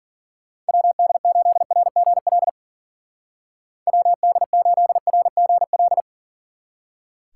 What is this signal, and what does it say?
Morse code, heard as a keyed tone, about 48 wpm, 700 Hz. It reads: WD8RGL WD8RGL
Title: Speed Racing - USA 2x3 Call Signs 40wpm